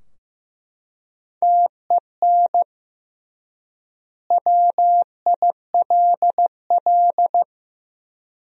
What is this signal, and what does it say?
Morse code, heard as a keyed tone, about 14 wpm, 700 Hz. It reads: TEN WILL
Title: 3x Word-Spacing - Sentences from Top 400 Words - 15wpm